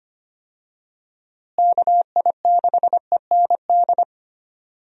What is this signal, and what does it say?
Morse code, heard as a keyed tone, about 25 wpm, 700 Hz. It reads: KI6END
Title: USA Call Signs - 6 Characters - Repeated 3x - 25wpm